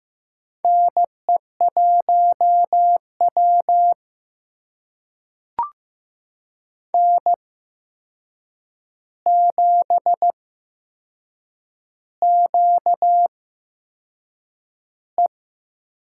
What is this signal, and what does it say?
Morse code, heard as a keyed tone, about 15 wpm, spaced as for 6 wpm, 700 Hz. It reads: NE1W N7QE
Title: USA Call Signs - 4 Characters - 8x Character Spacing - 15wpm